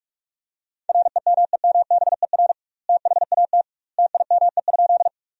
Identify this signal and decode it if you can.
Morse code, heard as a keyed tone, about 43 wpm, 700 Hz. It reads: REMEMBER THAT TIME?
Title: 2x Word-Spacing - Sentences from Top 400 Words - 45wpm